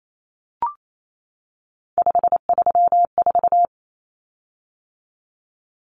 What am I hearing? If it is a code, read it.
Morse code, heard as a keyed tone, about 28 wpm, 700 Hz. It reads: 534
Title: Common Cut Numbers - 3-Digit 28wpm